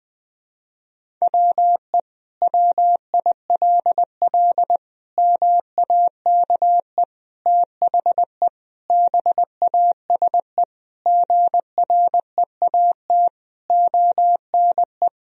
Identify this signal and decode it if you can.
Morse code, heard as a keyed tone, about 20 wpm, 700 Hz. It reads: WE WILL MAKE THE BASE GREAT ONE
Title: Sentences from Top 1000 Words - Review Part 1 - 20wpm